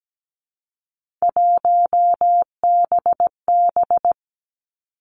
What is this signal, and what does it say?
Morse code, heard as a keyed tone, about 17 wpm, 700 Hz. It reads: 1BB
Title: Three Challenging Letter-Number Combinations 17wpm